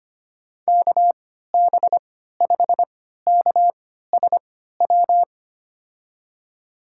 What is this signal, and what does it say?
Morse code, heard as a keyed tone, about 25 wpm, spaced as for 14 wpm, 700 Hz. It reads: KB5KSW
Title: USA Call Signs - 6 Characters - 3x Character Spacing - 25wpm